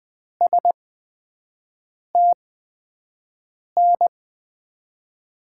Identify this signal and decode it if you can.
Morse code, heard as a keyed tone, about 20 wpm, spaced as for 5 wpm, 700 Hz. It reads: STN
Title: USA Call Signs - 6 Characters - 8x Character Spacing - 20wpm